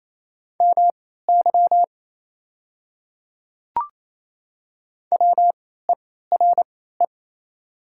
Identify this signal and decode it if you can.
Morse code, heard as a keyed tone, about 28 wpm, spaced as for 16 wpm, 690 Hz. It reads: MY WERE
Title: 3x Character Spacing - Top 100 Words - 28wpm